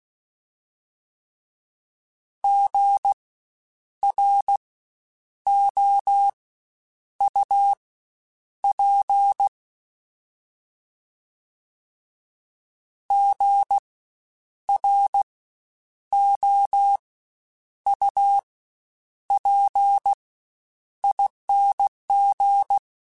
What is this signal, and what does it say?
Morse code, heard as a keyed tone, about 16 wpm, spaced as for 8 wpm, 800 Hz. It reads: GROUP GROUPING
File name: Spaced Alphas / Spaced Alphas mp3s / Gs sp @15